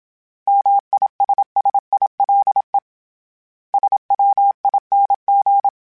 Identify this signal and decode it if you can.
Morse code, heard as a keyed tone, about 27 wpm, 800 Hz. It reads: MISSILE SWING
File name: Short Runs / Short Runs-mp3s / Words @25